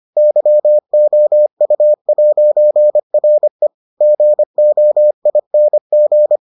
Morse code, heard as a keyed tone, about 25 wpm, 600 Hz. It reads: YOU'RE GOING